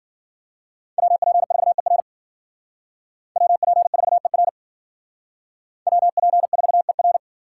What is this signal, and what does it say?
Morse code, heard as a keyed tone, about 53 wpm, 700 Hz. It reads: WP4ER WP4ER WP4ER